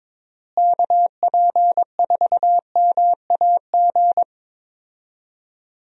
Morse code, heard as a keyed tone, about 22 wpm, 700 Hz. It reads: KP4MAG
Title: USA Call Signs - 6 Characters - Repeated 3x - 22wpm